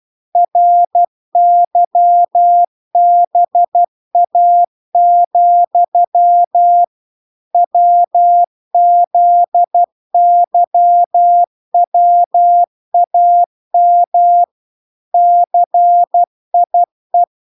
Morse code, heard as a keyed tone, about 12 wpm, 700 Hz. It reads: RYBA, WZYWAM CIE